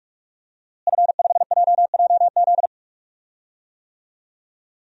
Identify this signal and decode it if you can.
Morse code, heard as a keyed tone, about 46 wpm, 700 Hz. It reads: F5JJZ
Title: DXCC-100 Call Signs - 5 Characters - Repeated 3x - 45wpm